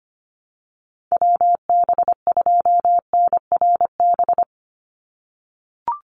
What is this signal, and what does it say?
Morse code, heard as a keyed tone, about 25 wpm, 700 Hz. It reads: WB2NRB
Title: USA Call Signs 25wpm